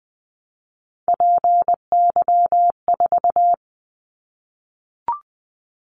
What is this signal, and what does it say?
Morse code, heard as a keyed tone, about 20 wpm, 700 Hz. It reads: PY4
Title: Callsign Prefixes for Top 50 DXCC - 20wpm